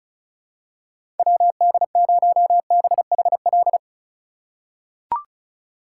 Morse code, heard as a keyed tone, about 36 wpm, 700 Hz. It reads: WD0BHL
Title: USA Call Signs - 6 Characters - 3x Character Spacing - 35wpm